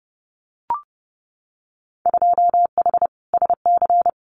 Morse code, heard as a keyed tone, about 30 wpm, 700 Hz. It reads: 2H SC
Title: Winter Field Day Exchanges 30wpm